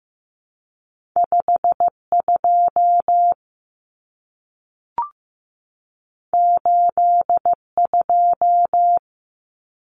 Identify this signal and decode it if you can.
Morse code, heard as a keyed tone, about 15 wpm, 700 Hz. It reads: 52 82
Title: Common Cut Numbers - 2-Digit 15wpm